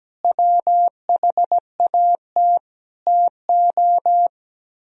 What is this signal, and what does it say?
Morse code, heard as a keyed tone, about 17 wpm, 700 Hz. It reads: WHAT TO
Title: Speed-Racing - Sets of 4 Words 17wpm